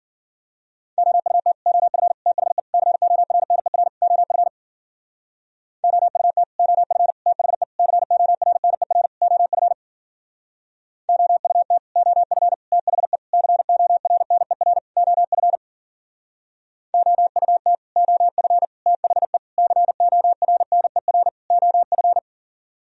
Morse code, heard as a keyed tone, about 53 wpm, 700 Hz. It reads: OUT OF THE CORNER OF OUT OF THE CORNER OF OUT OF THE CORNER OF OUT OF THE CORNER OF